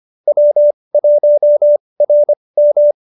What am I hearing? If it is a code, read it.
Morse code, heard as a keyed tone, about 25 wpm, spaced as for 20 wpm, 590 Hz. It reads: W1RM